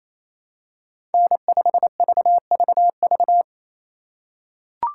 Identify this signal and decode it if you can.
Morse code, heard as a keyed tone, about 28 wpm, 700 Hz. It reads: N5VVV
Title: USA Call Signs - 5 Characters - 2x Character Spacing - 28wpm